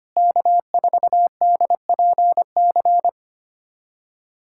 Morse code, heard as a keyed tone, about 25 wpm, 700 Hz. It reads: K4DPC
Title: USA Callsign DE Callsign - 5 Characters - 25wpm